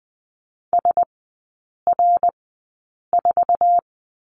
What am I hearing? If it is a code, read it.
Morse code, heard as a keyed tone, about 20 wpm, spaced as for 8 wpm, 700 Hz. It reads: SR4
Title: 121 Lesson 20e - Number 3 - Three Characters 20wpm